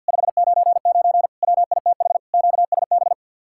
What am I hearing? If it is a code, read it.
Morse code, heard as a keyed tone, about 65 wpm, 700 Hz. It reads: 599 WITH QSB